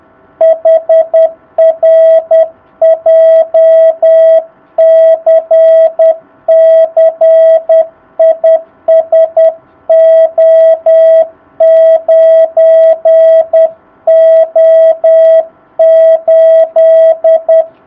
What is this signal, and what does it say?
Morse code, heard as a keyed tone, about 10 wpm, 660 Hz. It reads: HR JCC IS O9O8
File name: jh8cbh / 2019cw